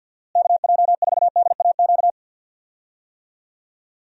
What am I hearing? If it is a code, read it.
Morse code, heard as a keyed tone, about 50 wpm, 700 Hz. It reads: KJ4DAQ
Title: USA Call Signs - 6 Characters - Repeated 3x - 50wpm